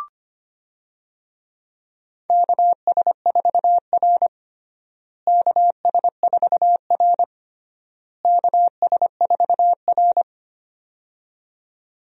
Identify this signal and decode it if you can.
Morse code, heard as a keyed tone, about 25 wpm, 700 Hz. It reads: KS4R KS4R KS4R
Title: USA Call Signs - 4 Characters - Repeated 3x - 25wpm